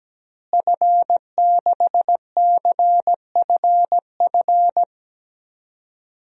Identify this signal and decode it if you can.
Morse code, heard as a keyed tone, about 17 wpm, 700 Hz. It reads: F6CFF
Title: DXCC-100 Call Signs - 5 Characters - Repeated 3x - 17wpm